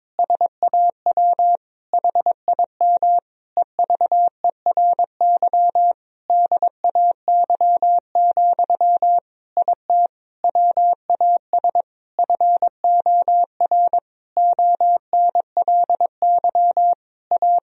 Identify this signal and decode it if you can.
Morse code, heard as a keyed tone, about 22 wpm, 700 Hz. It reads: SAW HIM EVERY DAY, IT WAS FOR ONLY A
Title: Sentences from Top 1000 Words - Review Part 3 - 22wpm